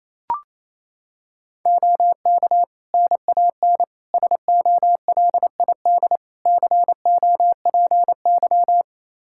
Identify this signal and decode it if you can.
Morse code, heard as a keyed tone, about 28 wpm, 700 Hz. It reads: OK NAN SOLID COPY
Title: Practice QSO - Exchange 3 - More Info - 28wpm